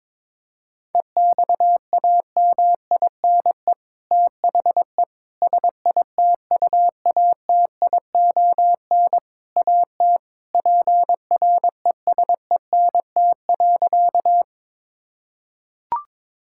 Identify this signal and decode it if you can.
Morse code, heard as a keyed tone, about 22 wpm, 700 Hz. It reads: EXAMINE THE SITUATION AT PRESENT.
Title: Sentences from Top 1600 Words 22wpm